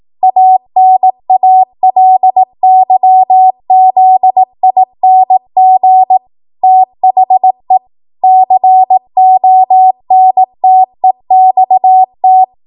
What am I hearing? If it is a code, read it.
Morse code, heard as a keyed tone, about 18 wpm, 750 Hz. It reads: ANALYZING THE CONTEXT
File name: Morse / 18WPM